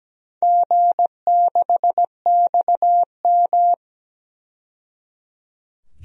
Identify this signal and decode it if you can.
Morse code, heard as a keyed tone, about 17 wpm, 700 Hz. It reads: G6XM
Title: DXCC-100 Call Signs - 4 Characters - 17wpm